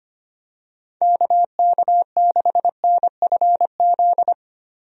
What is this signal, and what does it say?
Morse code, heard as a keyed tone, about 25 wpm, 700 Hz. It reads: KK6NFZ
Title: USA Call Signs - 6 Characters - Repeated 2x - 25wpm